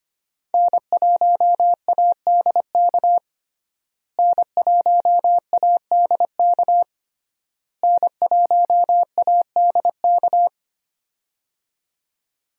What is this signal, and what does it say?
Morse code, heard as a keyed tone, about 25 wpm, 700 Hz. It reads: N1ADK N1ADK N1ADK
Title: USA Call Signs - 5 Characters - Repeated 3x - 25wpm